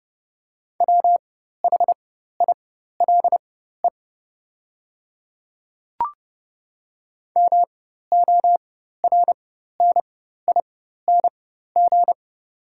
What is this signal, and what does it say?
Morse code, heard as a keyed tone, about 30 wpm, spaced as for 14 wpm, 700 Hz. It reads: WHILE MORNING
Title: 4x Character Spacing - Top 500 Words - 30wpm